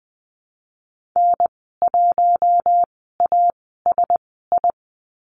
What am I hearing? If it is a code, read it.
Morse code, heard as a keyed tone, about 20 wpm, spaced as for 14 wpm, 700 Hz. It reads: N1ASI
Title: 088 Lesson 15g - Number 5 - Call Signs 20wpm